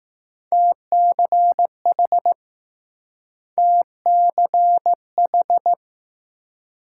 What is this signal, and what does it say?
Morse code, heard as a keyed tone, about 20 wpm, 700 Hz. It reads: TCH TCH